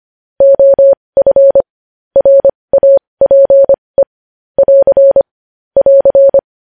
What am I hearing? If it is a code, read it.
Morse code, heard as a keyed tone, about 25 wpm, spaced as for 20 wpm, 550 Hz. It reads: OF RAPE <AR> <AR>
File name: podcast / News headlines-1x-25x20